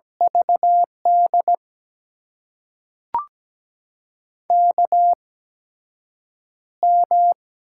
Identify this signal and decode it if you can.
Morse code, heard as a keyed tone, about 17 wpm, spaced as for 7 wpm, 700 Hz. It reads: VD KM
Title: USA Call Signs - 6 Characters - 8x Character Spacing - 17wpm